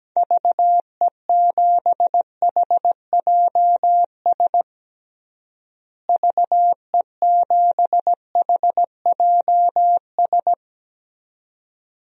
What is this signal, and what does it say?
Morse code, heard as a keyed tone, about 17 wpm, 700 Hz. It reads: VE7HJS VE7HJS
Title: DXCC-100 Call Signs - 6 Characters - Repeated 2x - 17wpm